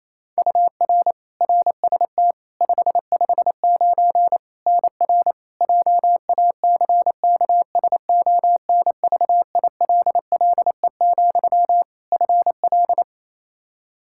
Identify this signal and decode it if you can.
Morse code, heard as a keyed tone, about 28 wpm, 700 Hz. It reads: UR RST 559 NR JACKSONVILLE, FL
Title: Practice QSO - Exchange 1 - Beginning - 28wpm